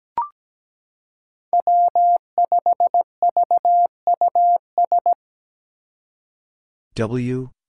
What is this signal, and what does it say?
Morse code, heard as a keyed tone, about 17 wpm, 700 Hz. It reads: W5VUS
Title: Five Challenging Letter-Number Combinations 17wpm